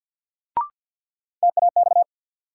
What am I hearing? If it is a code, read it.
Morse code, heard as a keyed tone, about 47 wpm, 700 Hz. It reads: TAX
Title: Sentences from Top 1000 Words - Review Part 5 - New Word Drill - 50wpm